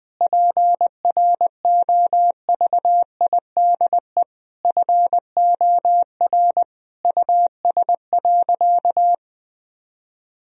Morse code, heard as a keyed tone, about 21 wpm, 700 Hz. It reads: PROVIDE FOR US.
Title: Sentences from Top 1000 Words - Review Part 3 - 20wpm